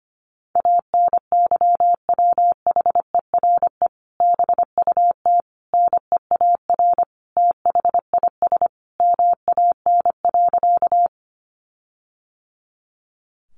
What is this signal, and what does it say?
Morse code, heard as a keyed tone, about 25 wpm, 700 Hz. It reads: ANYWHERE BUT NEAR THIS MAN.
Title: Sentences from Top 1400 Words 25wpm